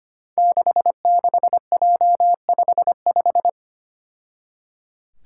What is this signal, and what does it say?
Morse code, heard as a keyed tone, about 25 wpm, 700 Hz. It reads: 66J55
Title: Five Challenging Letter-Number Combinations 25wpm